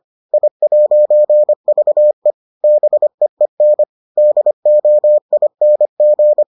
Morse code, heard as a keyed tone, about 25 wpm, 600 Hz. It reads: I'VE BEEN DOING